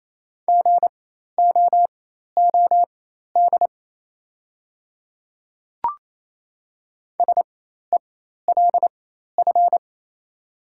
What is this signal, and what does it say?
Morse code, heard as a keyed tone, about 27 wpm, spaced as for 13 wpm, 700 Hz. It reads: GOOD SELF
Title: 4x Character Spacing - Top 300 Words - 28wpm